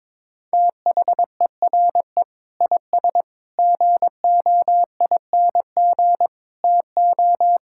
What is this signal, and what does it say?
Morse code, heard as a keyed tone, about 21 wpm, 700 Hz. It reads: THERE IS GOING TO
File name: Sentences with 7-Word Phrases - 22wpm